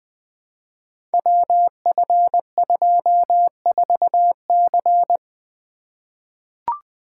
Morse code, heard as a keyed tone, about 20 wpm, 700 Hz. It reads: WF24C
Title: Code Groups - 5 Letter:Numbers - 20wpm